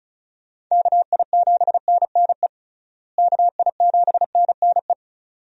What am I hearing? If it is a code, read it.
Morse code, heard as a keyed tone, about 36 wpm, 700 Hz. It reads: KI7NNE KI7NNE
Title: USA Call Signs - 6 Characters - Repeated 2x - 35wpm